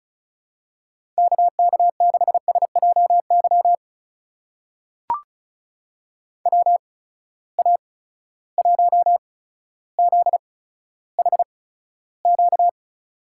Morse code, heard as a keyed tone, about 36 wpm, spaced as for 15 wpm, 700 Hz. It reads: KK6SJY WA1ZHQ
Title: USA Call Signs - 6 Characters - 8x Character Spacing - 35wpm